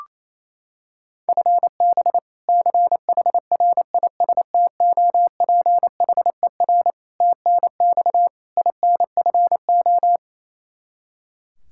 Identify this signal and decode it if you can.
Morse code, heard as a keyed tone, about 28 wpm, 700 Hz. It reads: FB CHRISTOPHER TNX INFO
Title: K1USN SST QSO Practice 28wpm